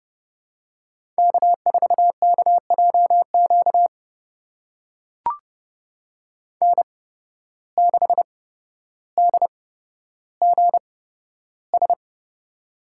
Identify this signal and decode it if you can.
Morse code, heard as a keyed tone, about 30 wpm, spaced as for 13 wpm, 700 Hz. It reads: K4KJQ N6DGS
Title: USA Call Signs - 5 Characters - 8x Character Spacing - 30wpm